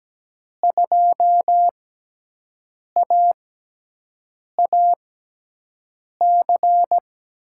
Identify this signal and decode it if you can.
Morse code, heard as a keyed tone, about 17 wpm, spaced as for 6 wpm, 700 Hz. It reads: 2AAC
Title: USA Call Signs - 5 Characters - 6x Character Spacing - 17wpm